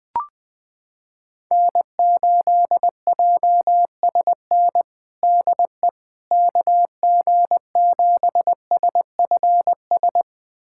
Morse code, heard as a keyed tone, about 21 wpm, 700 Hz. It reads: N8JSN DE KG7SFS
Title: USA Callsign DE Callsign 20wpm